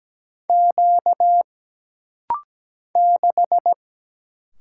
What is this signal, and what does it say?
Morse code, heard as a keyed tone, about 17 wpm, 700 Hz. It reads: Q 6
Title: Single Letter-Number - Rapid-Fire 17wpm